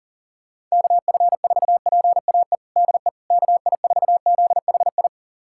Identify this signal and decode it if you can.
Morse code, heard as a keyed tone, about 40 wpm, 700 Hz. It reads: KF4PAE DE KI4ZHI
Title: USA Callsign DE Callsign - 6 Characters - 40wpm